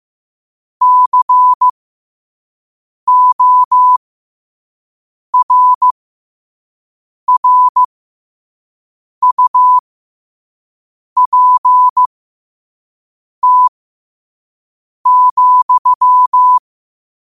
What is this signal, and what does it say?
Morse code, heard as a keyed tone, about 15 wpm, spaced as for 5 wpm, 1000 Hz. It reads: CORRUPT,
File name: qotdfiles / 5wpm